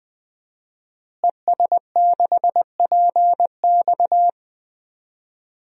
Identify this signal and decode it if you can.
Morse code, heard as a keyed tone, about 20 wpm, 700 Hz. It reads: ES6PX
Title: DXCC-100 Call Signs - 5 Characters - Repeated 3x - 20wpm